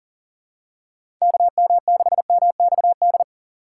Morse code, heard as a keyed tone, about 41 wpm, 700 Hz. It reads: KM6MXD
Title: USA Call Signs - 6 Characters - 6x Character Spacing - 40wpm